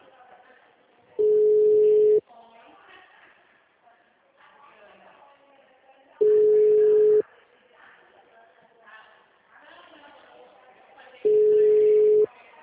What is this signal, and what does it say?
Ringback tone (Europe, 425 Hz), with speech.